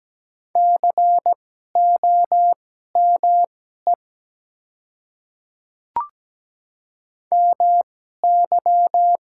Morse code, heard as a keyed tone, about 17 wpm, spaced as for 12 wpm, 700 Hz. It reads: COME MY
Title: 2x Character Spacing - Top 300 Words - 17wpm